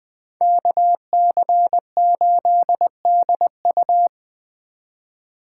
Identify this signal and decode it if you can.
Morse code, heard as a keyed tone, about 21 wpm, 700 Hz. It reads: KC8DU
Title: Speed Racing - USA 2x2 Call Signs 17wpm